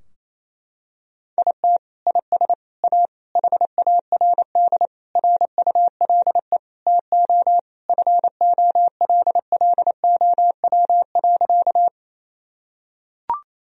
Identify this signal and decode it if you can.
Morse code, heard as a keyed tone, about 27 wpm, 700 Hz. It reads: IT IS A HARD RULE TO FOLLOW.
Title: Sentences from Top 1000 Words - Review Part 2 - 28wpm